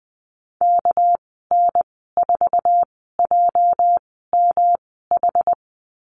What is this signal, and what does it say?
Morse code, heard as a keyed tone, about 19 wpm, spaced as for 14 wpm, 700 Hz. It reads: KN4JMH
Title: 207 Lesson 33g - Letter j - Call Signs 20wpm